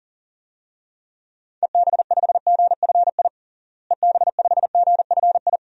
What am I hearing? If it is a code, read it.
Morse code, heard as a keyed tone, about 40 wpm, 700 Hz. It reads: EB5GFI EB5GFI
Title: DXCC-100 Call Signs - 6 Characters - Repeated 2x - 40wpm